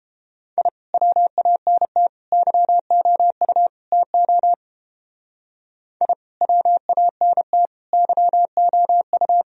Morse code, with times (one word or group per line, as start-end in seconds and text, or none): 0.6-0.7 I
0.9-2.1 WANT
2.3-3.7 YOU
3.9-4.5 TO
6.0-6.1 I
6.4-7.7 WANT
7.9-9.4 YOU